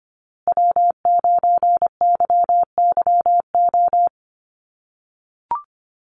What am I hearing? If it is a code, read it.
Morse code, heard as a keyed tone, about 25 wpm, 700 Hz. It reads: W9YYO
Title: USA Call Signs Repeated 2x 25wpm